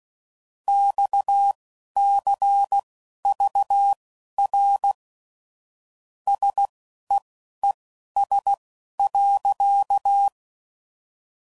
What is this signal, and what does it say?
Morse code, heard as a keyed tone, about 16 wpm, spaced as for 11 wpm, 800 Hz. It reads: XCVR SEES.
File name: CW Code Groups @15WPM / QSOs / WD0FBT Spaced